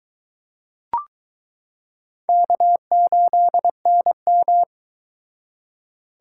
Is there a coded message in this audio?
Yes: Morse code from 2.3 to 4.6 s, heard as a keyed tone, about 22 wpm, 700 Hz. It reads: K8NM